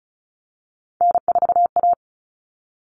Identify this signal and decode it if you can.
Morse code, heard as a keyed tone, about 34 wpm, 700 Hz. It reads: N4A
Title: All Cut Numbers - 3-Digit 35wpm